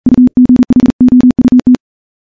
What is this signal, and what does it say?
Morse code, heard as a keyed tone, about 40 wpm, 250 Hz. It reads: UGHOUT
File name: morse / morse-training-40WPM